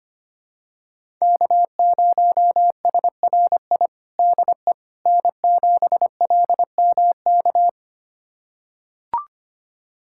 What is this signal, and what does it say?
Morse code, heard as a keyed tone, about 25 wpm, 700 Hz. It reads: K0SRI DE N7LMK
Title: USA Callsign DE Callsign - 5 Characters - 25wpm